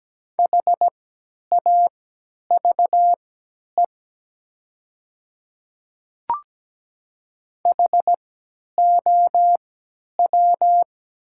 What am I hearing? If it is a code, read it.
Morse code, heard as a keyed tone, about 17 wpm, spaced as for 10 wpm, 700 Hz. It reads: HAVE HOW
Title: 3x Character Spacing - Top 100 Words - 17wpm